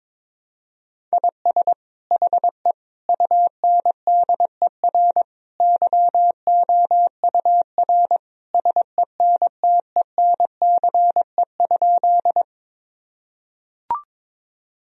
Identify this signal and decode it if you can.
Morse code, heard as a keyed tone, about 22 wpm, 700 Hz. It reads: IS HE UNDER YOUR SENTENCE?
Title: Sentences from Top 1000 Words - Review Part 1 - 22wpm